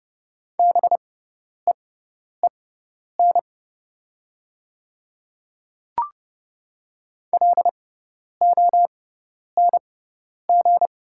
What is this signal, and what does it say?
Morse code, heard as a keyed tone, about 30 wpm, spaced as for 10 wpm, 700 Hz. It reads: BEEN LONG
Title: 6x Character Spacing - Top 200 Words - 30wpm